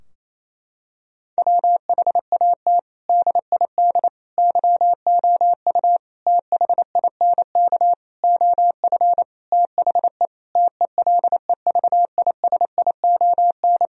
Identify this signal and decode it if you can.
Morse code, heard as a keyed tone, about 28 wpm, 700 Hz. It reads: WHAT DID YOU THINK OF THE TELEVISION
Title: Sentences from Top 1000 Words - Review Part 4 - 28wpm